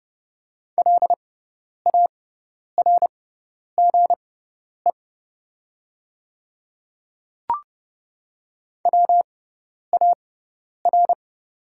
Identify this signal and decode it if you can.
Morse code, heard as a keyed tone, about 30 wpm, spaced as for 10 wpm, 690 Hz. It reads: LARGE WAR
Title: 6x Character Spacing - Top 500 Words - 30wpm